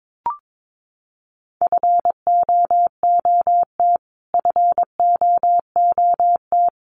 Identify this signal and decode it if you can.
Morse code, heard as a keyed tone, about 22 wpm, 700 Hz. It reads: FOOT FOOT
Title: Top 500 Words Repeated 22wpm